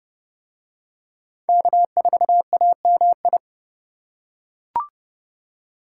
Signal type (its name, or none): Morse code